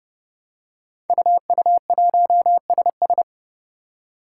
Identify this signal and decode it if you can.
Morse code, heard as a keyed tone, about 30 wpm, 700 Hz. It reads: UU1SS
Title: Five Challenging Letter-Number Combinations 30wpm